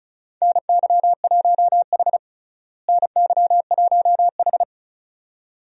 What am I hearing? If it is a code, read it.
Morse code, heard as a keyed tone, about 36 wpm, 700 Hz. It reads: NY1H NY1H
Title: USA Call Signs - 4 Characters - Repeated 2x - 35wpm